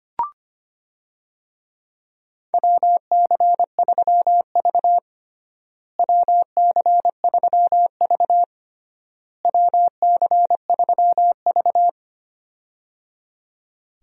Morse code, heard as a keyed tone, about 25 wpm, 700 Hz. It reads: WC3V WC3V WC3V
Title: USA Call Signs - 4 Characters - Repeated 3x - 25wpm